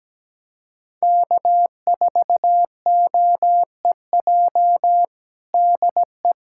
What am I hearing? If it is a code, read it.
Morse code, heard as a keyed tone, about 17 wpm, 700 Hz. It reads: K4OEJ DE
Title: USA Callsign DE Callsign - 5 Characters - 17wpm